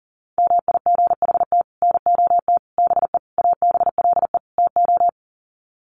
Morse code, heard as a keyed tone, about 40 wpm, 700 Hz. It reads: MIGHT NOT BE ABLE TO
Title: Sets of 5 Words Encore2 40wpm